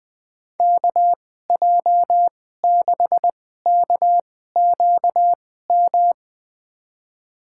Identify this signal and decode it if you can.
Morse code, heard as a keyed tone, about 20 wpm, spaced as for 15 wpm, 700 Hz. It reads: KJ6KQM
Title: USA Call Signs - 6 Characters - 2x Character Spacing - 20wpm